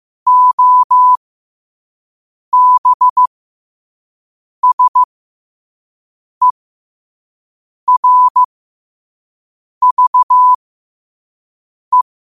Morse code, heard as a keyed tone, about 15 wpm, spaced as for 5 wpm, 1000 Hz. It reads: OBSERVE
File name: qotdfiles / 5wpm